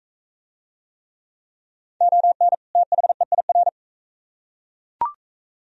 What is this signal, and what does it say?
Morse code, heard as a keyed tone, about 40 wpm, 700 Hz. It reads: ON THEIR